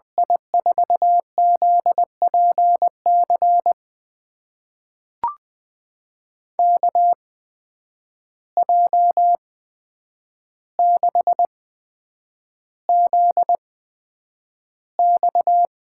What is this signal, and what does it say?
Morse code, heard as a keyed tone, about 20 wpm, spaced as for 9 wpm, 700 Hz. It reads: I4ZPC KJ6ZX
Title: USA Call Signs - 6 Characters - 8x Character Spacing - 20wpm